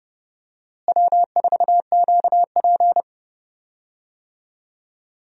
Morse code, heard as a keyed tone, about 29 wpm, 700 Hz. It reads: W4QP
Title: USA Call Signs - 4 Characters - Repeated 3x - 30wpm